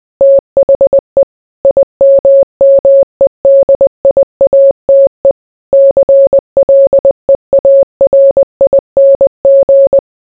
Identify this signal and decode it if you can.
Morse code, heard as a keyed tone, about 20 wpm, 550 Hz. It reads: THE IMMEDIATE CLEARING